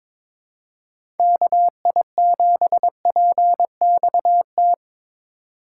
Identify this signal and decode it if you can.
Morse code, heard as a keyed tone, about 22 wpm, 700 Hz. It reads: KI7PXT
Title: USA Call Signs - 2x Character Spacing - 22wpm